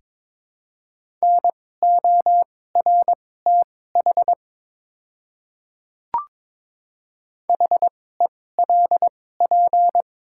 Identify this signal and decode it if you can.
Morse code, heard as a keyed tone, about 22 wpm, spaced as for 16 wpm, 700 Hz. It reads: NORTH HELP
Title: 2x Character Spacing - Top 300 Words - 22wpm